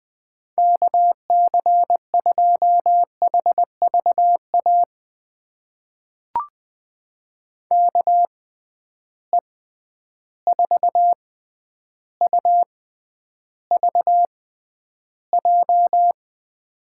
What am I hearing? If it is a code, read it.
Morse code, heard as a keyed tone, about 20 wpm, spaced as for 10 wpm, 700 Hz. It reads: KC2HVA KE4UVJ
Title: USA Call Signs - 6 Characters - 6x Character Spacing - 20wpm